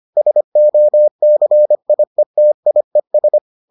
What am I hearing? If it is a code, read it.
Morse code, heard as a keyed tone, about 25 wpm, 600 Hz. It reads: SOCIETIES